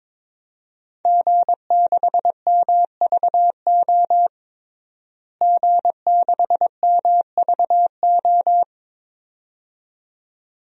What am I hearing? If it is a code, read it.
Morse code, heard as a keyed tone, about 22 wpm, 700 Hz. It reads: G6MVO G6MVO
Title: DXCC-100 Call Signs - 5 Characters - Repeated 2x - 22wpm